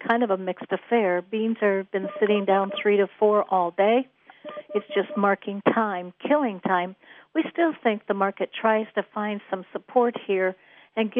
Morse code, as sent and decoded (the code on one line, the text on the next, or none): none